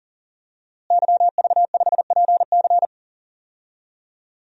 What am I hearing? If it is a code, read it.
Morse code, heard as a keyed tone, about 40 wpm, 700 Hz. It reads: YV5PC
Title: DXCC-100 Call Signs - 5 Characters - Repeated 3x - 40wpm